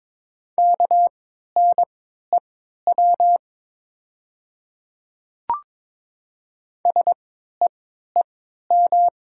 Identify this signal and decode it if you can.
Morse code, heard as a keyed tone, about 22 wpm, spaced as for 12 wpm, 700 Hz. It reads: KNEW SEEM
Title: 3x Character Spacing - Top 500 Words - 22wpm